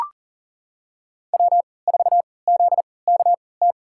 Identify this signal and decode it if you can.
Morse code, heard as a keyed tone, about 40 wpm, spaced as for 23 wpm, 700 Hz. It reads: W4ZKT
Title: USA Call Signs - 5 Characters - 3x Character Spacing - 40wpm